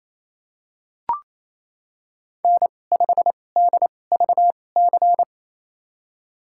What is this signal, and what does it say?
Morse code, heard as a keyed tone, about 28 wpm, spaced as for 20 wpm, 700 Hz. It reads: N5DVC
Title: USA Call Signs - 2x Character Spacing - 28wpm